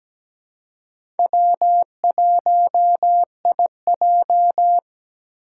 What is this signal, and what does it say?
Morse code, heard as a keyed tone, about 17 wpm, 700 Hz. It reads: W1IJ
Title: USA Call Signs - 2x Character Spacing - 17wpm